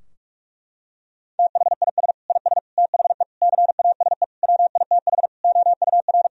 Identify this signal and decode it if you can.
Morse code, heard as a keyed tone, about 46 wpm, 700 Hz. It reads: THIS IS THE CASE WITH OUR
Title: Sentences from Top 1000 Words - Review Part 3 - 45wpm